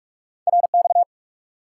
Morse code, heard as a keyed tone, about 46 wpm, 700 Hz. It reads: RX
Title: Top 90 QSO Elements 45wpm